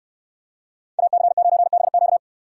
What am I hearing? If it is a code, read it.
Morse code, heard as a keyed tone, about 69 wpm, 700 Hz. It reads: AC9BQ